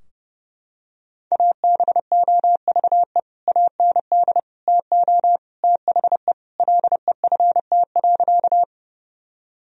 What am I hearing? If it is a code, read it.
Morse code, heard as a keyed tone, about 32 wpm, 700 Hz. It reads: ABOVE AND TO THE LEFT.
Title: Sentences from Top 1000 Words - Review Part 2 - 30wpm